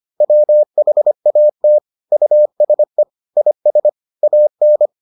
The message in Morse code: WHAT USE IS AN

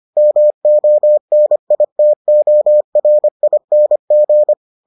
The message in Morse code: MONITORING

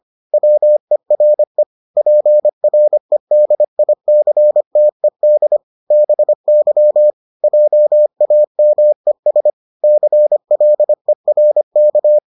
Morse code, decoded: WERE PREDICTED BY JAMES CLERK

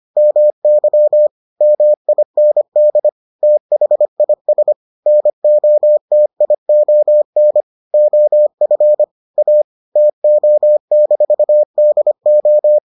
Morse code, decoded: MY MIND THIS NOTION OF A TO-DO

